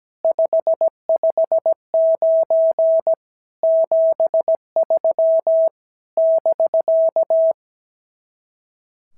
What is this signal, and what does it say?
Morse code, heard as a keyed tone, about 17 wpm, 650 Hz. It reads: 559 73 <BK>
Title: SOTA QSO Practice 17wpm